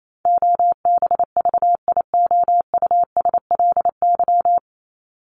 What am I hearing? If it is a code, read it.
Morse code, heard as a keyed tone, about 28 wpm, 700 Hz. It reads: OBVIOUSLY
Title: LY Suffix - Top 100 Words with Base Words 28wpm